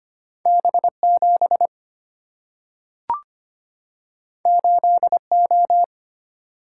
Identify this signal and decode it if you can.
Morse code, heard as a keyed tone, about 25 wpm, 700 Hz. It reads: B7 8O
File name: Two Letter-Number Combinations - 25wpm